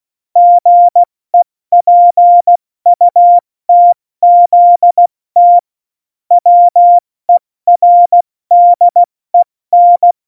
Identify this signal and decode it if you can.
Morse code, heard as a keyed tone, about 16 wpm, 700 Hz. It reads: GEPUTZT WERDEN
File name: Morsekurs / Witze / Witz 03.Dezember 2019, 70 F80